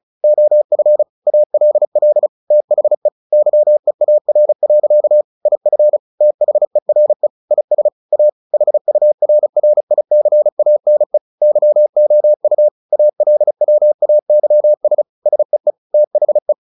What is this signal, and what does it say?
Morse code, heard as a keyed tone, about 34 wpm, 600 Hz. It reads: OF ALL THE YEAR. IF THERE IS A HURRICANE YOU ALWAYS SEE THE
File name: the old man and the sea / the old man and the sea 1 35wpm